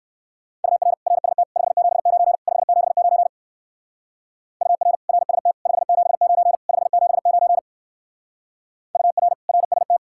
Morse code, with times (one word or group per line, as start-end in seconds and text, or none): 0.6-0.9 UR
1.1-1.4 RST
1.6-2.4 579
2.5-3.3 579
4.6-5.0 UR
5.1-5.5 RST
5.7-6.6 579
6.7-7.6 579
9.0-9.3 UR
9.5-10.0 RST